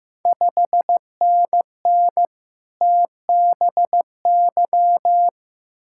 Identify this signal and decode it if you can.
Morse code, heard as a keyed tone, about 15 wpm, 700 Hz. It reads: 5NN TBY
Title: Ontario QSO Party Exchange 15wpm